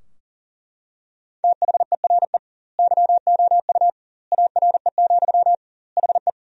Morse code, heard as a keyed tone, about 37 wpm, 700 Hz. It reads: THERE YOU ARE, HE
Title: 2x Word-Spacing - Sentences from Top 100 Words - 40wpm